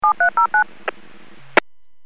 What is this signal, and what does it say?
DTMF, heard as telephone touch tones, tones of about 110 ms. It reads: 7309
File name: WPS / 00856 00-21 WPS 451.25 WPS